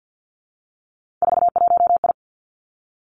Morse code, heard as a keyed tone, about 50 wpm, 700 Hz. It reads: 41I
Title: Three Challenging Letter-Number Combinations 50wpm